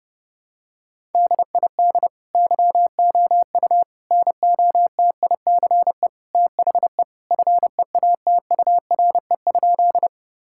Morse code, heard as a keyed tone, about 32 wpm, 700 Hz. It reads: DID YOU NOTICE THE FEATURE?
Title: Sentences from Top 1000 Words - Review Part 5 - 30wpm